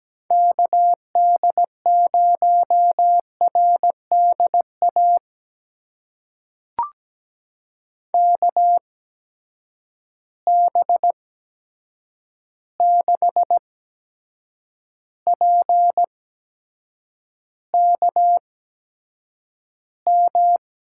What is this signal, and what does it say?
Morse code, heard as a keyed tone, about 17 wpm, spaced as for 7 wpm, 700 Hz. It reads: KD0RDA KB6PKM
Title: USA Call Signs - 8x Character Spacing - 17wpm